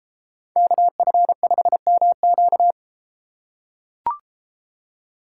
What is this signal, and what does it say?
Morse code, heard as a keyed tone, about 32 wpm, 700 Hz. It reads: KF5MQ